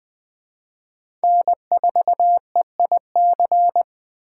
Morse code, heard as a keyed tone, about 20 wpm, 700 Hz. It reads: N4EIC
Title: USA Call Signs - 5 Characters - 2x Character Spacing - 20wpm